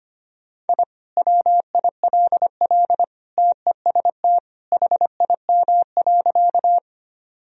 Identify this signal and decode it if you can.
Morse code, heard as a keyed tone, about 25 wpm, 700 Hz. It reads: I WILL TEST HIM.
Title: Sentences from Top 1000 Words - Review Part 3 - 25wpm